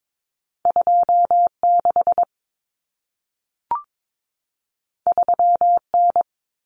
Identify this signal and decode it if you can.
Morse code, heard as a keyed tone, about 22 wpm, 700 Hz. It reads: 26 3N
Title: Common Cut Numbers - 2-Digit 22wpm